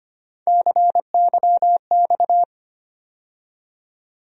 Morse code, heard as a keyed tone, about 25 wpm, 700 Hz. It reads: CYX
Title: USA Call Sign Postfixes - 3 Letters 25wpm